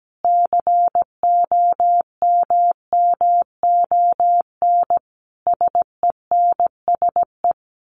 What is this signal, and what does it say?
Morse code, heard as a keyed tone, about 17 wpm, 700 Hz. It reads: COMMON SENSE
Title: Sentences from Top 1200 Words 17wpm